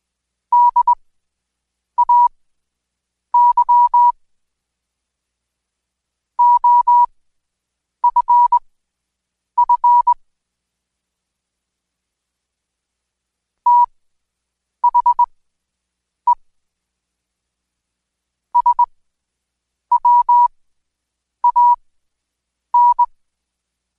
Morse code, heard as a keyed tone, about 21 wpm, spaced as for 7 wpm, 970 Hz. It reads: DAY OFF THE SWAN
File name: Code Quick / AudioDownld2 / Seg112-Day17-DayDrill 17